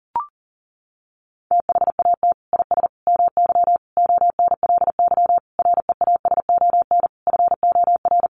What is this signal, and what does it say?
Morse code, heard as a keyed tone, about 40 wpm, 700 Hz. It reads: THAT IS MY ONLY REASON FOR
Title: Sentences from Top 700 Words 40wpm